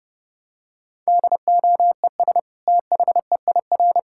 Morse code, heard as a keyed tone, about 30 wpm, 700 Hz. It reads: DOES THEIR
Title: Sentences from Top 1000 Words - Review Part 5 - 30wpm